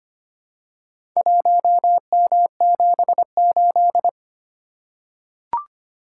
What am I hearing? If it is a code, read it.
Morse code, heard as a keyed tone, about 25 wpm, 700 Hz. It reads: 1M78
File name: practice / 25wpm / Four Letter-Number Combinations - 25wpm